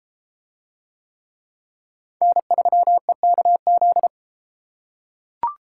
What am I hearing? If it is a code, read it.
Morse code, heard as a keyed tone, about 33 wpm, 700 Hz. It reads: N3EKZ